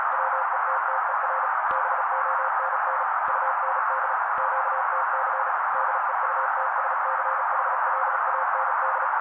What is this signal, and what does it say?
Morse code, heard as a keyed tone, about 35 wpm, 530 Hz. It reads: WETTER BONN UND QTC DEUTSCHLAND
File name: daily / 35wpm noise